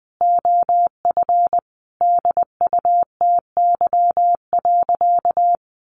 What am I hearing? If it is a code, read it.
Morse code, heard as a keyed tone, about 20 wpm, 700 Hz. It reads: OF DUTY.
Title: Sentences from Top 1500 Words 20wpm